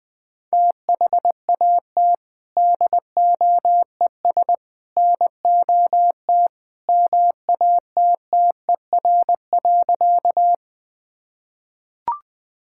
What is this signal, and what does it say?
Morse code, heard as a keyed tone, about 20 wpm, 700 Hz. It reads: THAT DOES NOT MATTER.
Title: Sentences with 3-Word Phrases - 20wpm